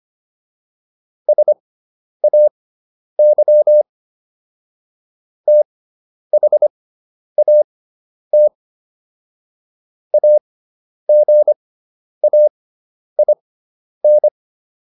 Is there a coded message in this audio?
Yes: Morse code from 1.3 to 14.3 s, heard as a keyed tone, about 25 wpm, spaced as for 10 wpm, 600 Hz. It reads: SAY THAT AGAIN